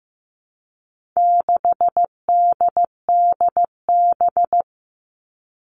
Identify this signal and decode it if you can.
Morse code, heard as a keyed tone, about 15 wpm, 700 Hz. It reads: 6DDB
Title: All Cut Numbers - 4-Digit 15wpm